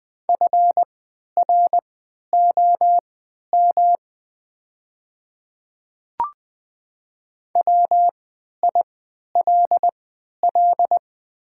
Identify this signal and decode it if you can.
Morse code, heard as a keyed tone, about 20 wpm, spaced as for 11 wpm, 700 Hz. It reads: FROM WILL
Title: 3x Character Spacing - Top 100 Words - 20wpm